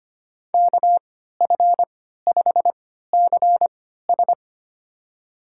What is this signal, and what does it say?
Morse code, heard as a keyed tone, about 25 wpm, spaced as for 14 wpm, 700 Hz. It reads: KF5CS
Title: USA Call Signs - 5 Characters - 3x Character Spacing - 25wpm